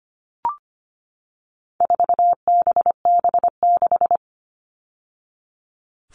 Morse code, heard as a keyed tone, about 25 wpm, 700 Hz. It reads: 4BB6
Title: All Cut Numbers - 4-Digit 25wpm